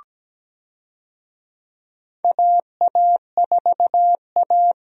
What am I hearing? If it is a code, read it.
Morse code, heard as a keyed tone, about 17 wpm, 700 Hz. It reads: AA4A
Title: USA Call Signs - 4 Characters - Repeated 2x - 17wpm